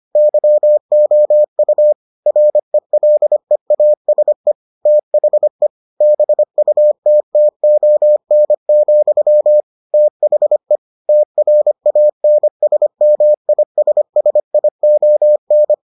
Morse code, heard as a keyed tone, about 25 wpm, 600 Hz. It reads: YOU RELEASE THE BUTTON, THE TRANSMISSION